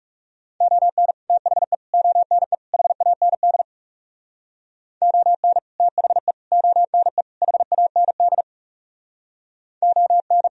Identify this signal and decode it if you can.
Morse code, heard as a keyed tone, about 38 wpm, 700 Hz. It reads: ON THE ONE HAND ON THE ONE HAND ON